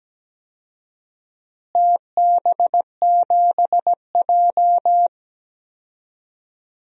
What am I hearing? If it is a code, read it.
Morse code, heard as a keyed tone, about 17 wpm, 700 Hz. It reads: TB7J
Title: DXCC-100 Call Signs - 4 Characters - Repeated 2x - 17wpm